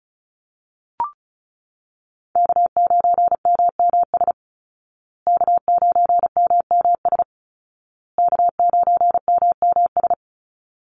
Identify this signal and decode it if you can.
Morse code, heard as a keyed tone, about 34 wpm, 700 Hz. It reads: K9MMS K9MMS K9MMS
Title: Call Signs Repeated 2x - 35wpm